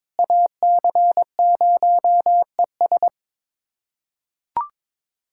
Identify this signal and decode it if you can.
Morse code, heard as a keyed tone, about 22 wpm, 700 Hz. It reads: AC0ES
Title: USA Callsign DE Callsign 22wpm